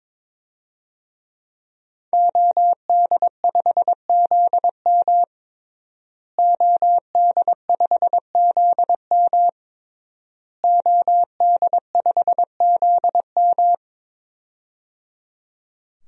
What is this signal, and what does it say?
Morse code, heard as a keyed tone, about 22 wpm, 700 Hz. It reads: OD5ZM OD5ZM OD5ZM
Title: DXCC-100 Call Signs - 5 Characters - Repeated 3x - 22wpm